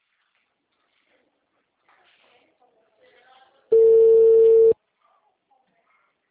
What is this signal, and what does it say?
Ringback tone (China, 450 Hz), with speech.